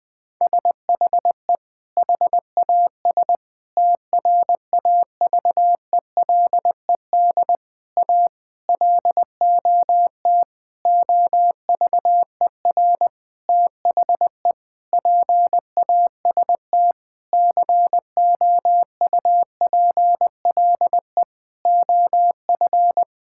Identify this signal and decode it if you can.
Morse code, heard as a keyed tone, about 19 wpm, 700 Hz. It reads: SHE HAS TRAVELED A LOT OVER THE PAST COUPLE OF
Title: Sentences with 6-Word Phrases - 20wpm